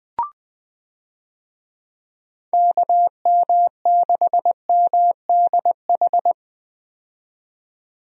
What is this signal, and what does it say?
Morse code, heard as a keyed tone, about 20 wpm, 700 Hz. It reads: KM6MDH
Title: USA Call Signs - 6 Characters - 20wpm